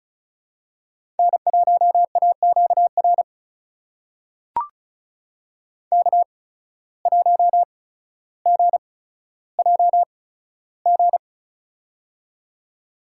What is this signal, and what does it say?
Morse code, heard as a keyed tone, about 36 wpm, spaced as for 15 wpm, 700 Hz. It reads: N1AQR K1GJG
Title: USA Call Signs - 5 Characters - 8x Character Spacing - 35wpm